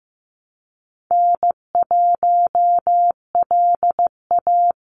Morse code, heard as a keyed tone, about 15 wpm, 700 Hz. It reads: N1LA
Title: USA Call Signs 15wpm